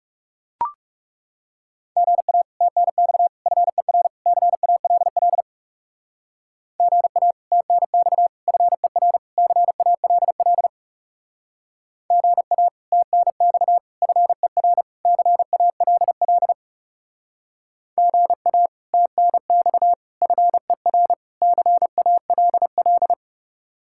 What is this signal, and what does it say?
Morse code, heard as a keyed tone, about 37 wpm, 700 Hz. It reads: GA TNX FER CALL GA TNX FER CALL GA TNX FER CALL GA TNX FER CALL
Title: Speed-Racing Common QSO Phrases - 30wpm